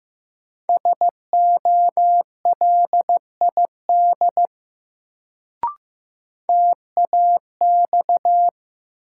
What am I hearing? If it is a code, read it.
Morse code, heard as a keyed tone, about 15 wpm, 700 Hz. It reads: SOLID TAX
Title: Sentences from Top 1500 Words - New Word Drill Alt - 15wpm